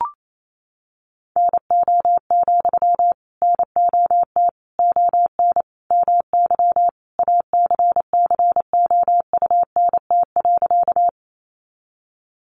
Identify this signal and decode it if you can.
Morse code, heard as a keyed tone, about 28 wpm, 700 Hz. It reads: NO, NOT ON MY ACCOUNT.
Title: Sentences from Top 900 Words 28wpm